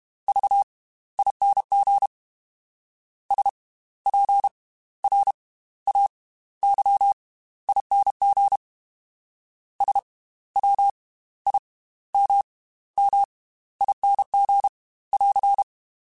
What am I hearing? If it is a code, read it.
Morse code, heard as a keyed tone, about 32 wpm, spaced as for 16 wpm, 800 Hz. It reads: VING SPRAYING SWIMMING <AR>